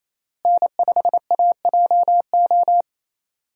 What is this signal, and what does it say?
Morse code, heard as a keyed tone, about 28 wpm, 700 Hz. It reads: N5AJO
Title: USA Call Signs - 5 Characters - Repeated 3x - 28wpm